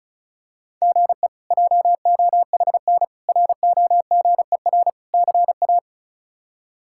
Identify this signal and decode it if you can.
Morse code, heard as a keyed tone, about 34 wpm, 700 Hz. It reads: GE JOHN ROGER CA
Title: K1USN SST QSO Practice - 35wpm